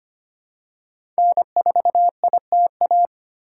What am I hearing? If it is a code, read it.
Morse code, heard as a keyed tone, about 25 wpm, 700 Hz. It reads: N4ITA
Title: USA Call Signs - 5 Characters - 2x Character Spacing - 25wpm